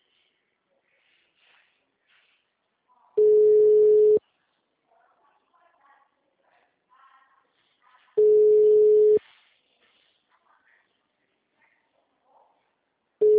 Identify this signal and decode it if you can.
Ringback tone (Europe, 425 Hz).